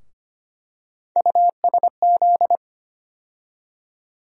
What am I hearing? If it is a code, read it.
Morse code, heard as a keyed tone, about 25 wpm, 700 Hz. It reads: USZ
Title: USA Call Sign Postfixes - 3 Letters 25wpm